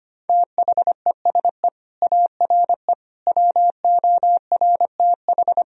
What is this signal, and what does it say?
Morse code, heard as a keyed tone, about 25 wpm, 700 Hz. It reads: THESE ARE WORTH
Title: Sentences from Top 1000 Words - Review Part 3 - 25wpm